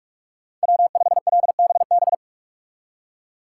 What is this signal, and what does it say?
Morse code, heard as a keyed tone, about 41 wpm, 700 Hz. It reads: W5LBB
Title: USA Call Signs - 5 Characters - 4x Character Spacing - 45wpm